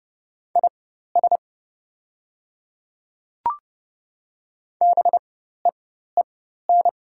Morse code, heard as a keyed tone, about 30 wpm, spaced as for 14 wpm, 700 Hz. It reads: IS BEEN
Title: 4x Character Spacing - Top 100 Words - 30wpm